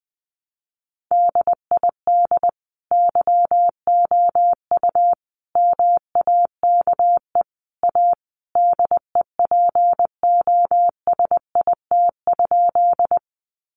Morse code, heard as a keyed tone, about 20 wpm, 700 Hz. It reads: DID YOU MAKE A DEPOSIT?
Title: Sentences from Top 1300 Words 20wpm